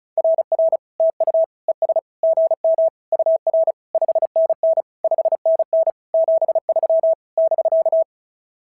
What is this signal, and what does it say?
Morse code, heard as a keyed tone, about 36 wpm, 650 Hz. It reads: RR TU ES GM UR 5NN 5NN 73 <BK>
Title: SOTA QSO Practice 35wpm